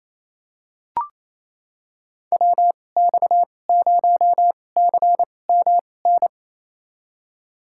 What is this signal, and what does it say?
Morse code, heard as a keyed tone, about 28 wpm, spaced as for 20 wpm, 700 Hz. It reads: WX0CMN
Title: USA Call Signs - 6 Characters - 2x Character Spacing - 28wpm